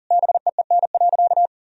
Morse code, heard as a keyed tone, about 40 wpm, 700 Hz. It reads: BEEN.
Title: Sentences from Top 1000 Words - Review Part 4 - 40wpm